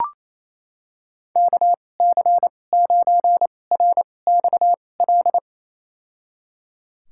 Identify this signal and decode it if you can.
Morse code, heard as a keyed tone, about 28 wpm, spaced as for 20 wpm, 700 Hz. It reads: KC9RXL
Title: USA Call Signs - 6 Characters - 2x Character Spacing - 28wpm